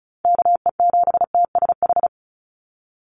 Morse code, heard as a keyed tone, about 34 wpm, 700 Hz. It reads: KE7TSH
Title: USA Call Signs 35wpm